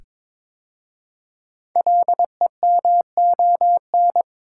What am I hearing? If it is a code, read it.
Morse code, heard as a keyed tone, about 22 wpm, 700 Hz. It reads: LEMON
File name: 5 Letter Words - First 3 Letters Given - 22wpm